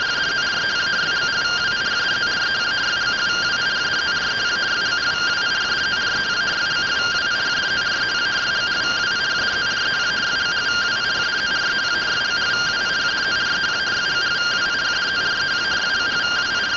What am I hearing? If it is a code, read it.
Two-tone teleprinter signal, about 170 Hz shift.